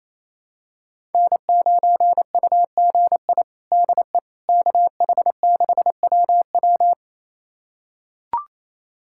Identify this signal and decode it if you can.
Morse code, heard as a keyed tone, about 28 wpm, 700 Hz. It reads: N9UGI DE KH6WW
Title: USA Callsign DE Callsign - 5 Characters - 28wpm